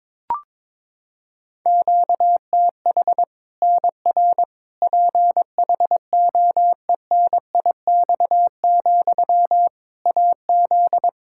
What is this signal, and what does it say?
Morse code, heard as a keyed tone, about 21 wpm, 700 Hz. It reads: QTH NR PHOENIX, AZ
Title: K1USN SST QSO Practice 22wpm